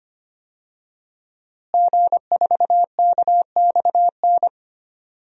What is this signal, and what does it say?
Morse code, heard as a keyed tone, about 25 wpm, 700 Hz. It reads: G4KXN
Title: DXCC-100 Call Signs - 5 Characters - Repeated 2x - 25wpm